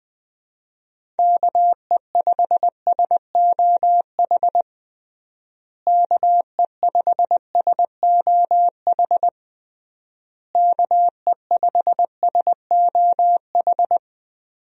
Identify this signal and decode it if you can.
Morse code, heard as a keyed tone, about 20 wpm, 700 Hz. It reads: KE5SOH KE5SOH KE5SOH
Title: USA Call Signs - 6 Characters - Repeated 3x - 20wpm